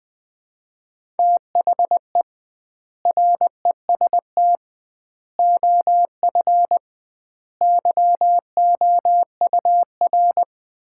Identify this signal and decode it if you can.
Morse code, heard as a keyed tone, about 20 wpm, 700 Hz. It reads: THE REST OF YOUR
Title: 2x Word-Spacing - Sets of 5 Words - 20wpm